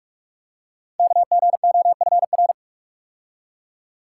Morse code, heard as a keyed tone, about 44 wpm, 700 Hz. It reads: KGOFR
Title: Code Groups - 5 Letters - 45wpm